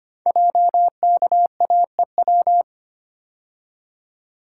Morse code, heard as a keyed tone, about 25 wpm, 700 Hz. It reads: JKAEW
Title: Code Groups - 5 Letters - 25wpm